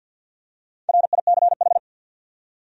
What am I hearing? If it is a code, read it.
Morse code, heard as a keyed tone, about 53 wpm, 700 Hz. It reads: RICH
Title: USA Common Names - 50wpm